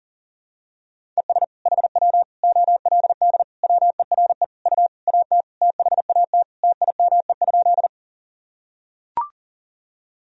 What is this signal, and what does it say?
Morse code, heard as a keyed tone, about 40 wpm, 700 Hz. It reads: ES HW OLD WERE U AT THAT TIME?
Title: Sentences from Top 200 Words and Abbreviations 40wpm